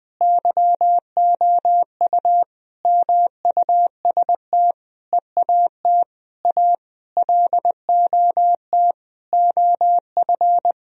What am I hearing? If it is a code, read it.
Morse code, heard as a keyed tone, about 20 wpm, 700 Hz. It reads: YOU MUST EAT A LOT OF
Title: Sentences from Top 1000 Words - Review Part 3 - 20wpm